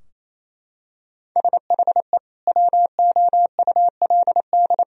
Morse code, heard as a keyed tone, about 28 wpm, 700 Hz. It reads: SHE WOULD
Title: Sentences from Top 1000 Words - Review Part 4 - 28wpm